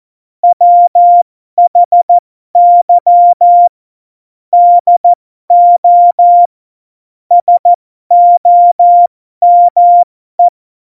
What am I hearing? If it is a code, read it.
Morse code, heard as a keyed tone, about 14 wpm, spaced as for 12 wpm, 700 Hz. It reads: WHY DO SOME
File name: Morsekurs / Witze